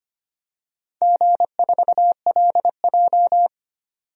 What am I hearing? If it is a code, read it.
Morse code, heard as a keyed tone, about 25 wpm, 700 Hz. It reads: G4LJ
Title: DXCC-100 Call Signs - 4 Characters - 25wpm